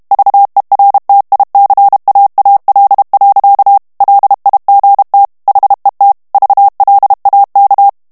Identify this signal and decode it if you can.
Morse code, heard as a keyed tone, about 32 wpm, 800 Hz. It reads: VERTICAAL. LIGT HET VLAK
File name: HOLLANDAIS texte clair 1 vitesse 30 mots minute